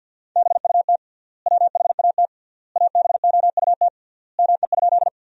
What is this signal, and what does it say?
Morse code, heard as a keyed tone, about 50 wpm, 700 Hz. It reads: BUT WHAT ABOUT ME?
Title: 3x Word-Spacing - Sentences from Top 200 Words - 50wpm